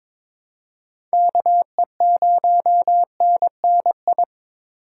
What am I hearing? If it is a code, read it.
Morse code, heard as a keyed tone, about 22 wpm, 700 Hz. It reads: KE0NNI
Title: USA Call Signs - 4x Character Spacing - 22wpm